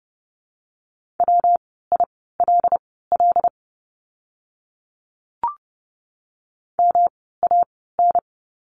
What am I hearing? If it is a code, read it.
Morse code, heard as a keyed tone, about 30 wpm, spaced as for 17 wpm, 700 Hz. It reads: WILL MAN
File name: course / 30:10wpm / 115 Lesson 19f - Letter w - Words 30wpm